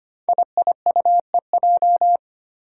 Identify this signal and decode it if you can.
Morse code, heard as a keyed tone, about 25 wpm, 700 Hz. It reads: IIUEJ
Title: Five Challenging Letter-Number Combinations 25wpm